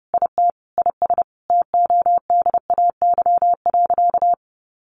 Morse code, heard as a keyed tone, about 30 wpm, 700 Hz. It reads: IT IS TODAY.
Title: Sentences from Top 1500 Words 30wpm